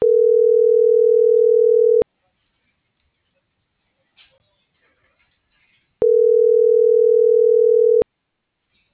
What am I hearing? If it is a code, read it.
Ringback tone (North America), 440 + 480 Hz.